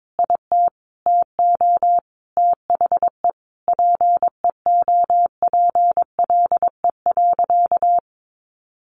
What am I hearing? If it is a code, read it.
Morse code, heard as a keyed tone, about 22 wpm, 700 Hz. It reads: IT TO THE PEOPLE.
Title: Sentences from Top 1500 Words 22wpm